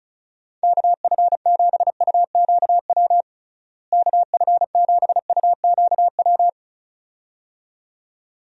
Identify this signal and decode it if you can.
Morse code, heard as a keyed tone, about 36 wpm, 700 Hz. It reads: KF7UQW KF7UQW
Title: USA Call Signs - 6 Characters - Repeated 2x - 35wpm